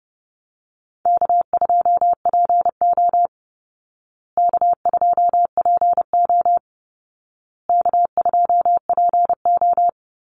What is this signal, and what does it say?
Morse code, heard as a keyed tone, about 30 wpm, 700 Hz. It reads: K2PO K2PO K2PO
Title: Easy Call Signs Repeated 3x - 30wpm